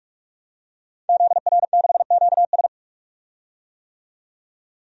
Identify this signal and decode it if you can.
Morse code, heard as a keyed tone, about 41 wpm, 700 Hz. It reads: ZR6QS
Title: DXCC-100 Call Signs - 5 Characters - 45wpm